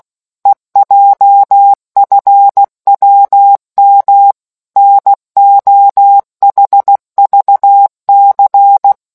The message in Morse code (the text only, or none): EJFWM NOHVC